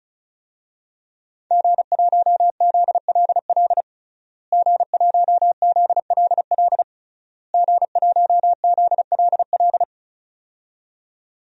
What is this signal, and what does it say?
Morse code, heard as a keyed tone, about 36 wpm, 700 Hz. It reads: G1ZLL G1ZLL G1ZLL
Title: DXCC-100 Call Signs - 5 Characters - Repeated 3x - 35wpm